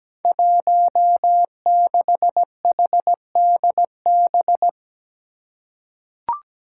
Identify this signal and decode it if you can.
Morse code, heard as a keyed tone, about 17 wpm, 700 Hz. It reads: 16HDB
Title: Five Challenging Letter-Number Combinations 17wpm